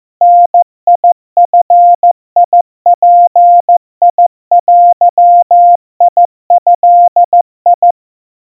Morse code (Japanese Wabun code), heard as a keyed tone, about 14 wpm, 700 Hz. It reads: ダヂヅデド
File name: kawa / Morse sample / aiueo 50 cpm 700 Hz Crispy